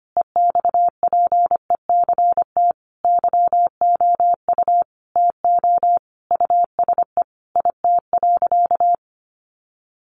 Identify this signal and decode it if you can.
Morse code, heard as a keyed tone, about 25 wpm, 700 Hz. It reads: EXPECT YOU TO USE IT.
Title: Sentences from Top 1100 Words 25wpm